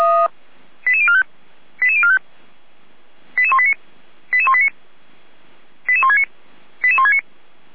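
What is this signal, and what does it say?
Five-tone selective call (ZVEI), heard as quick runs of five tones, tones of about 70 ms. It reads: 88045 88045 88189 88189 88169 88169
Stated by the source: ZVEI five-tone selective call received with an SDR.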